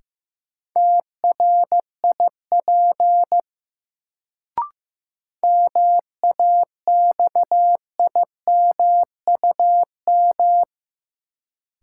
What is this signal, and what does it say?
Morse code, heard as a keyed tone, about 15 wpm, 700 Hz. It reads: TRIP MAXIMUM
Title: Sentences from Top 1300 Words - New Word Drill Alt - 15wpm